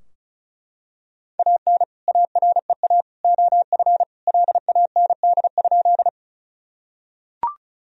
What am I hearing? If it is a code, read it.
Morse code, heard as a keyed tone, about 36 wpm, 700 Hz. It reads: AN AREA OF LAND?
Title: Sentences from Top 1000 Words - Review Part 2 - 35wpm